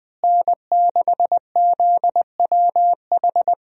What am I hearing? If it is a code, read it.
Morse code, heard as a keyed tone, about 20 wpm, 700 Hz. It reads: N6ZWH
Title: USA Call Signs - 5 Characters - Repeated 2x - 20wpm